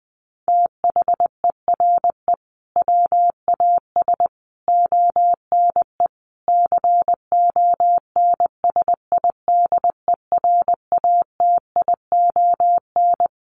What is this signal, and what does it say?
Morse code, heard as a keyed tone, about 20 wpm, 700 Hz. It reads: THERE WAS ONE CONSIDERATION
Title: Sentences from Top 1500 Words 20wpm